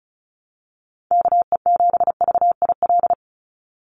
Morse code, heard as a keyed tone, about 34 wpm, 700 Hz. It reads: KE7VIL
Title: USA Call Signs 35wpm